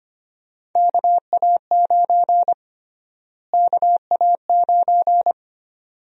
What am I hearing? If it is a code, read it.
Morse code, heard as a keyed tone, about 25 wpm, 700 Hz. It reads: KA9 KA9
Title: USA Call Sign Prefixes - Repeated 2x - 25wpm